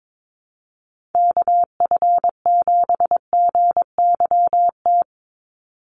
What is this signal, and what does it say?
Morse code, heard as a keyed tone, about 22 wpm, 700 Hz. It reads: KF7GYT
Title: USA Call Signs 22wpm